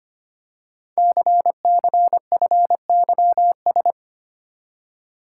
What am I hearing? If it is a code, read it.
Morse code, heard as a keyed tone, about 25 wpm, 700 Hz. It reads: CCFYS
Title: Code Groups - 5 Letters - 25wpm